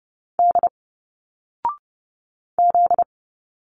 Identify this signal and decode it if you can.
Morse code, heard as a keyed tone, about 30 wpm, 700 Hz. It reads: D Z
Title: Single Letters 30wpm